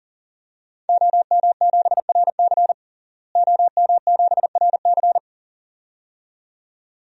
Morse code, heard as a keyed tone, about 40 wpm, 700 Hz. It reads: OM7RC OM7RC
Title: DXCC-100 Call Signs - 5 Characters - Repeated 2x - 40wpm